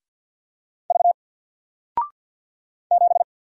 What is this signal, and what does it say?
Morse code, heard as a keyed tone, about 50 wpm, 700 Hz. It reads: V 7
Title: Single Character (Letter:Number:Punctuation:Prosign) Alt 50wpm